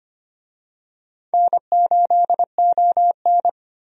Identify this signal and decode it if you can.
Morse code, heard as a keyed tone, about 25 wpm, 700 Hz. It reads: N8ON
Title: USA Call Signs - 4 Characters - 25wpm